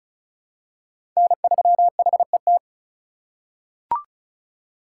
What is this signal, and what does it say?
Morse code, heard as a keyed tone, about 36 wpm, 700 Hz. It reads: N3HET
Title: USA Call Signs - 5 Characters - 3x Character Spacing - 35wpm